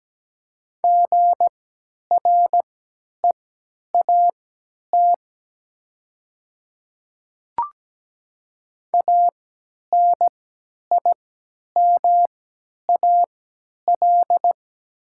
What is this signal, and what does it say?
Morse code, heard as a keyed tone, about 17 wpm, spaced as for 10 wpm, 700 Hz. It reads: GREAT ANIMAL
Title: 3x Character Spacing - Top 300 Words - 17wpm